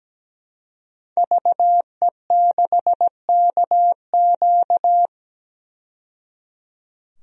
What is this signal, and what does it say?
Morse code, heard as a keyed tone, about 17 wpm, 700 Hz. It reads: VE6KQ
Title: DXCC-100 Call Signs - 5 Characters - 17wpm